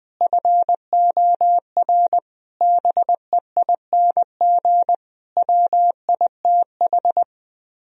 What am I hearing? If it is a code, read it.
Morse code, heard as a keyed tone, about 20 wpm, 700 Hz. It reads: FOR BEING WITH
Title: Speed-Racing - Sets of 5 Words 17wpm